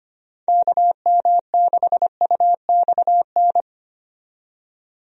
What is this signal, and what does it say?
Morse code, heard as a keyed tone, about 25 wpm, 700 Hz. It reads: KM6UXN
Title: USA Call Signs - 6 Characters - 25wpm